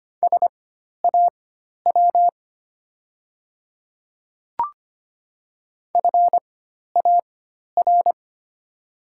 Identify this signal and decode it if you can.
Morse code, heard as a keyed tone, about 25 wpm, spaced as for 12 wpm, 700 Hz. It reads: SAW FAR
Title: 4x Character Spacing - Top 500 Words - 25wpm